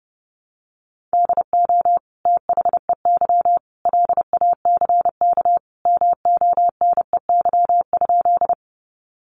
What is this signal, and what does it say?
Morse code, heard as a keyed tone, about 28 wpm, 700 Hz. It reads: DO THEY LACK MONEY?
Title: Sentences from Top 1100 Words 30wpm